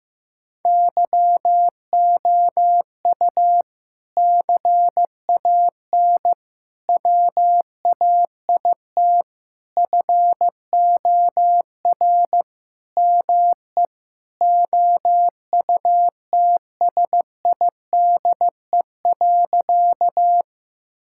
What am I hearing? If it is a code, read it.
Morse code, heard as a keyed tone, about 14 wpm, 700 Hz. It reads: YOU CAN WAIT FOR ME OUTSIDE.
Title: Sentences from Top 1000 Words - Review Part 6 - 15wpm